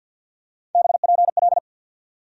Morse code, heard as a keyed tone, about 53 wpm, 700 Hz. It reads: BPL
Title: USA Call Sign Postfixes - 3 Letters 50wpm